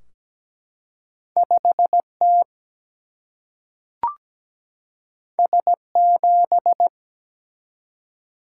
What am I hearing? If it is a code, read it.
Morse code, heard as a keyed tone, about 17 wpm, 700 Hz. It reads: 5T S7
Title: Single-Letter Single-Number 17wpm